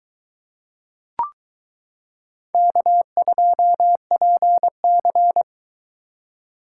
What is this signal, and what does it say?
Morse code, heard as a keyed tone, about 22 wpm, 700 Hz. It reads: K2PC